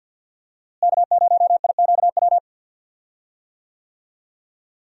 Morse code, heard as a keyed tone, about 52 wpm, 700 Hz. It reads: K0IQW
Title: USA Call Signs - 5 Characters - Repeated 2x - 50wpm